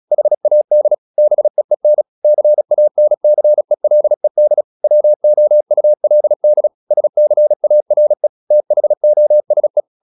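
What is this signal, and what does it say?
Morse code, heard as a keyed tone, about 36 wpm, 600 Hz. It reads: HAD BEEN CANCELED WOULD SCARE THOSE
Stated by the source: Morse practice at 36 wpm, text of Ken Follett's novel Never, from CW med Gnister.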